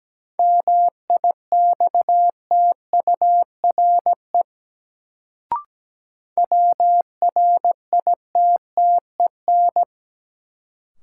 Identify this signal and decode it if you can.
Morse code, heard as a keyed tone, about 17 wpm, 700 Hz. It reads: MIXTURE WRITTEN
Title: Sentences from Top 1500 Words - New Word Drill Alt - 17wpm